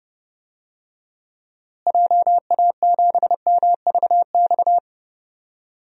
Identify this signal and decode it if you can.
Morse code, heard as a keyed tone, about 30 wpm, 700 Hz. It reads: JA7MVX
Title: DXCC-100 Call Signs - 6 Characters - Repeated 3x - 30wpm